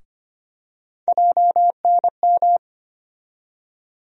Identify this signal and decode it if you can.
Morse code, heard as a keyed tone, about 25 wpm, 700 Hz. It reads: JNM
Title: USA Call Sign Postfixes - 3 Letters 25wpm